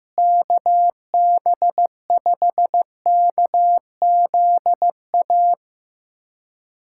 Morse code, heard as a keyed tone, about 15 wpm, 700 Hz. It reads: KB5KZA
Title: USA Callsign DE Callsign - 6 Characters - 15wpm